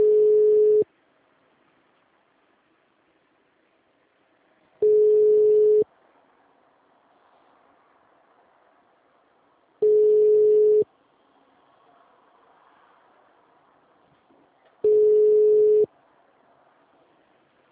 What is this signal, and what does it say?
Ringback tone (Europe, 425 Hz).
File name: out- -2001- - - .39